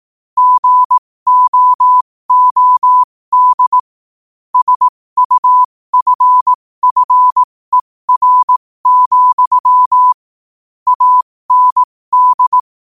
Morse code, heard as a keyed tone, about 18 wpm, spaced as for 15 wpm, 1000 Hz. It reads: GOOD SUFFER, AND
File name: qotdfiles / 15wpm